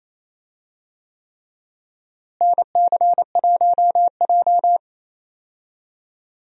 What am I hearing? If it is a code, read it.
Morse code, heard as a keyed tone, about 28 wpm, 700 Hz. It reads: NC1J
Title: USA Call Signs - 4 Characters - Repeated 3x - 28wpm